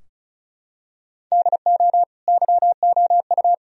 Morse code, heard as a keyed tone, about 34 wpm, 700 Hz. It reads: DO YOU
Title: Sentences from Top 1000 Words - Review Part 4 - 35wpm